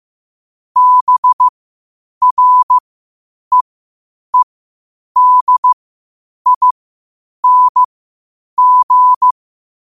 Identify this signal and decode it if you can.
Morse code, heard as a keyed tone, about 15 wpm, spaced as for 8 wpm, 1000 Hz. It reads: BREEDING
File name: qotdfiles / 8wpm